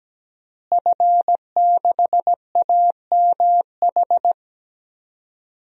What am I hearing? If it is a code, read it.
Morse code, heard as a keyed tone, about 17 wpm, 700 Hz. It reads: F6AMH
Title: DXCC-100 Call Signs - 5 Characters - Repeated 2x - 17wpm